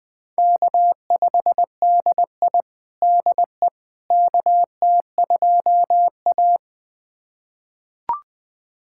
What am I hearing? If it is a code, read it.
Morse code, heard as a keyed tone, about 20 wpm, 700 Hz. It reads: K5DI DE KT2A
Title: USA Callsign DE Callsign - 4 Characters - 20wpm